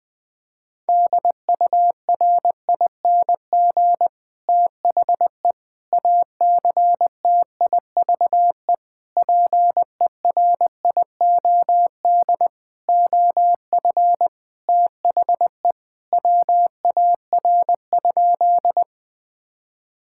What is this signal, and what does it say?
Morse code, heard as a keyed tone, about 20 wpm, 700 Hz. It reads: DURING THE ACTIVE PERIOD OF THE WAR?
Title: Sentences from Top 1000 Words - Review Part 6 - 20wpm